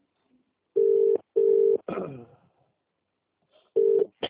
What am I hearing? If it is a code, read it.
Ringback tone (UK), 400 + 450 Hz.